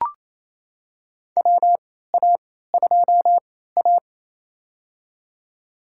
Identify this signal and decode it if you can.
Morse code, heard as a keyed tone, about 28 wpm, spaced as for 16 wpm, 700 Hz. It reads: WA2A
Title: USA Call Signs - 3x Character Spacing - 28wpm